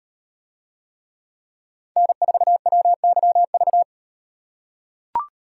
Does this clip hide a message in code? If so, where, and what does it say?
Yes: Morse code from 2.0 to 3.8 s, heard as a keyed tone, about 34 wpm, 700 Hz. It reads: N4WYV